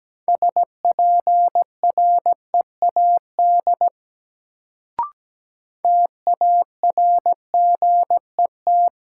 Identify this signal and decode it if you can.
Morse code, heard as a keyed tone, about 17 wpm, 700 Hz. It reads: SPREAD TARGET
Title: Sentences from Top 1200 Words - New Word Drill - 17wpm